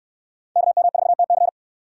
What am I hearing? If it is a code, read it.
Morse code, heard as a keyed tone, about 64 wpm, 700 Hz. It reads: KM4TC